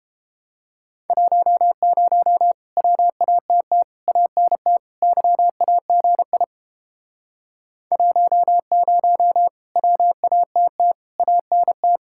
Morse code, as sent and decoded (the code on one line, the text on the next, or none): .---- ----- / .-- .- - - / .- -. - / -.-- .- --. .. / .---- ----- / .-- .- - - / .- -. -
10 WATT ANT YAGI 10 WATT ANT